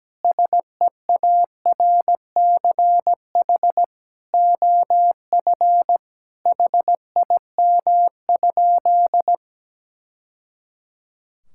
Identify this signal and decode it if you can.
Morse code, heard as a keyed tone, about 17 wpm, 700 Hz. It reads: SEARCH OF HIM?
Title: Sentences from Top 1000 Words - Review Part 5 - 17wpm